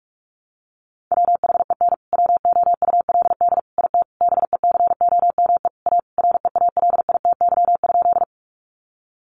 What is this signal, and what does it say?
Morse code, heard as a keyed tone, about 44 wpm, 700 Hz. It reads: WHEN WOULD IT BECOME A REALITY?
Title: Sentences from Top 1200 Words 45wpm